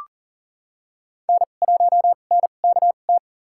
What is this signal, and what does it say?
Morse code, heard as a keyed tone, about 40 wpm, spaced as for 29 wpm, 700 Hz. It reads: N1NKT
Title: USA Call Signs - 5 Characters - 2x Character Spacing - 40wpm